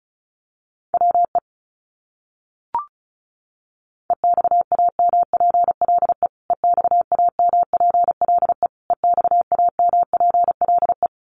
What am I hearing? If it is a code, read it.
Morse code, heard as a keyed tone, about 34 wpm, 700 Hz. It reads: WE EXAMPLE EXAMPLE EXAMPLE
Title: Top 300 Words Repeated 35wpm